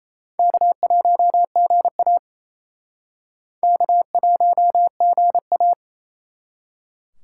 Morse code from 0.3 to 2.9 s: K1GA

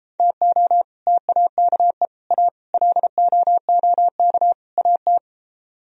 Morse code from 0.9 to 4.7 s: TAKE A LOOK

